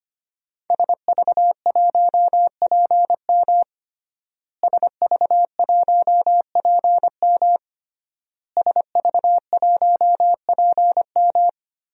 Morse code, heard as a keyed tone, about 25 wpm, 700 Hz. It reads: SV1PM SV1PM SV1PM
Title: DXCC-100 Call Signs - 5 Characters - Repeated 3x - 25wpm